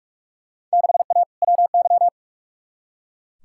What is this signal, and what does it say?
Morse code, heard as a keyed tone, about 41 wpm, 700 Hz. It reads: 6A WY
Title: Field Day Exchanges Encore - 45wpm